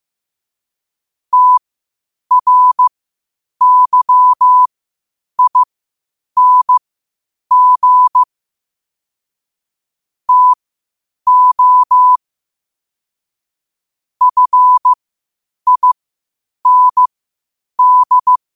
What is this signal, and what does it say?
Morse code, heard as a keyed tone, about 15 wpm, spaced as for 8 wpm, 1000 Hz. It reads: TRYING TO FIND
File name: qotdfiles / 8wpm